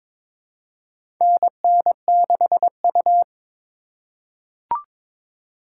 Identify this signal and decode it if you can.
Morse code, heard as a keyed tone, about 22 wpm, 700 Hz. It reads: NN6U
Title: USA Call Signs - 4 Characters - 4x Character Spacing - 22wpm